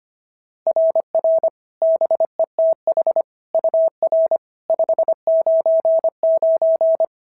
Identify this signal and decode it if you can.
Morse code, heard as a keyed tone, about 22 wpm, 650 Hz. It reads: RR BETH UR 599
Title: SOTA QSO Practice 25wpm